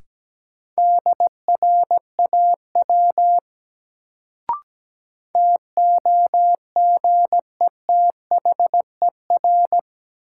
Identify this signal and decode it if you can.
Morse code, heard as a keyed tone, about 17 wpm, 700 Hz. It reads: DRAW TOGETHER
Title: Sentences from Top 300 Words - New Word Drill - 17wpm